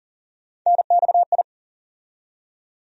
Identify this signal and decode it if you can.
Morse code, heard as a keyed tone, about 40 wpm, 700 Hz. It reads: NXI
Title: USA Call Sign Postfixes - 3 Letters 40wpm